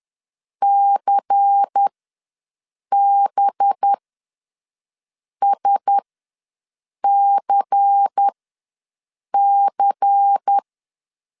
Morse code, heard as a keyed tone, about 10 wpm, spaced as for 6 wpm, 800 Hz. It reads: CBSCC
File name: LEZIONI / 5 LBVC / 5 LBVC ws 10wpm 3733